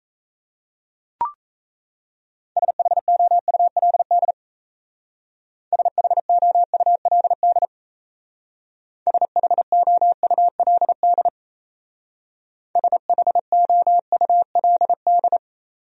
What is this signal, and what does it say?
Morse code, heard as a keyed tone, about 33 wpm, 700 Hz. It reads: SHOULD SHOULD SHOULD SHOULD
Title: Speed-Racing Top 500 Words 28wpm